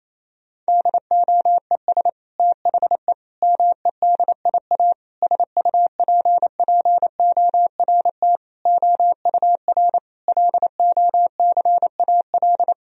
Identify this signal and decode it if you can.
Morse code, heard as a keyed tone, about 28 wpm, 700 Hz. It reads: DOES THE MEDIA SUPPORT OUR LOCAL
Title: Sentences from Top 1000 Words - Review Part 4 - 28wpm